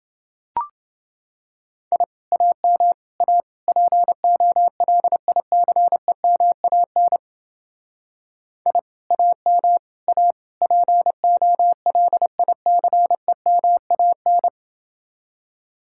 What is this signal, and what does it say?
Morse code, heard as a keyed tone, about 29 wpm, 700 Hz. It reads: I AM A POLICEMAN I AM A POLICEMAN